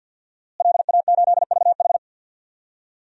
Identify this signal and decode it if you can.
Morse code, heard as a keyed tone, about 46 wpm, 700 Hz. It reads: LA8VH
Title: DXCC-100 Call Signs - 5 Characters - Repeated 2x - 50wpm